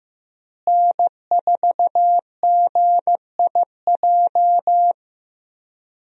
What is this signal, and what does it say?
Morse code, heard as a keyed tone, about 15 wpm, 700 Hz. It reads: N4GIJ
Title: USA Call Signs - 5 Characters - Repeated 2x - 15wpm